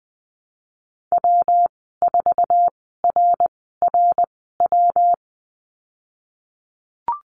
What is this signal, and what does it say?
Morse code, heard as a keyed tone, about 20 wpm, spaced as for 15 wpm, 700 Hz. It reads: W4RRW
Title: 167 Lesson 27g - Number 7 - Call Signs 20wpm